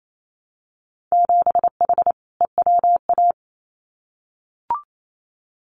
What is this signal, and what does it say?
Morse code, heard as a keyed tone, about 28 wpm, 700 Hz. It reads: 7H EWA
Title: Winter Field Day Exchanges Encore 28wpm